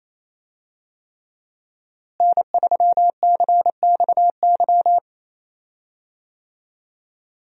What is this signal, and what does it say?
Morse code, heard as a keyed tone, about 28 wpm, 700 Hz. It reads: N3CXY
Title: USA Call Signs - 5 Characters - Repeated 3x - 28wpm